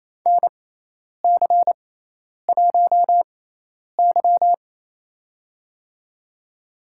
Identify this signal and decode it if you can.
Morse code, heard as a keyed tone, about 28 wpm, spaced as for 10 wpm, 700 Hz. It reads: NC1Y
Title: USA Call Signs - 4 Characters - 6x Character Spacing - 28wpm